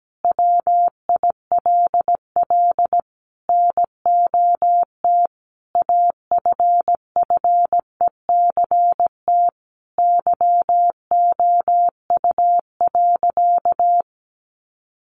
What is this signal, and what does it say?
Morse code, heard as a keyed tone, about 17 wpm, 700 Hz. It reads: WILL NOT AFFECT YOU.
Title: Sentences from Top 900 Words 17wpm